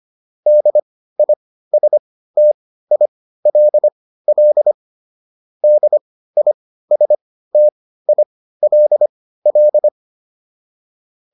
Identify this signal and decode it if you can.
Morse code, heard as a keyed tone, about 25 wpm, spaced as for 15 wpm, 600 Hz. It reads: DISTILL DISTILL